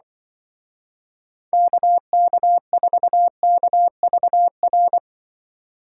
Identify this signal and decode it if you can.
Morse code, heard as a keyed tone, about 23 wpm, 700 Hz. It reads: KK4KVR
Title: Speed Racing - USA Call Signs 20wpm